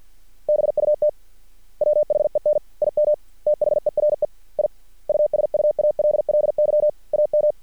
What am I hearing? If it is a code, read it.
Morse code, heard as a keyed tone, about 50 wpm, 600 Hz. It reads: BUT WHEN IM THERE I USUALLY AM
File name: archived / CRSnet